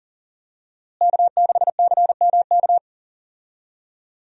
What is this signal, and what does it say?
Morse code, heard as a keyed tone, about 40 wpm, 700 Hz. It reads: K6CMK
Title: USA Call Signs - 5 Characters - Repeated 3x - 40wpm